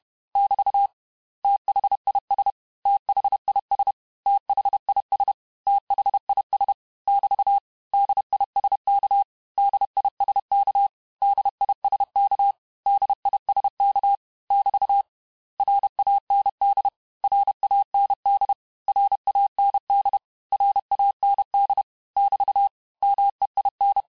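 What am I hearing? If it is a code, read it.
Morse code, heard as a keyed tone, about 31 wpm, 800 Hz. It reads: <BT> THIS THIS THIS THIS <BT> DISK DISK DISK DISK <BT> RAND RAND RAND RAND <BT> MEIN
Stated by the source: Morse lesson of words, each word repeated 4 times, at 29 words per minute.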